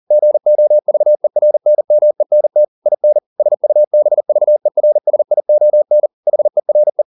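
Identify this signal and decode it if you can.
Morse code, heard as a keyed tone, about 40 wpm, 600 Hz. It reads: GOVERNMENT IN SUBVERSION HERE
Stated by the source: Morse code at 40 wpm, practice text, CW med Gnister.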